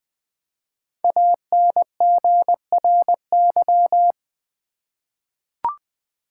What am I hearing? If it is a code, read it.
Morse code, heard as a keyed tone, about 20 wpm, 700 Hz. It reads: ANGRY
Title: 5 Letter Words - First 3 Letters Given - 20wpm